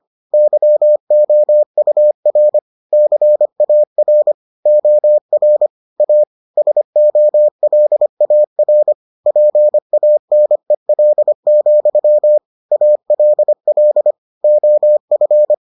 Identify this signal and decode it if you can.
Morse code, heard as a keyed tone, about 25 wpm, 600 Hz. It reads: YOUR CAR OR A SOLAR PANEL, ALL OF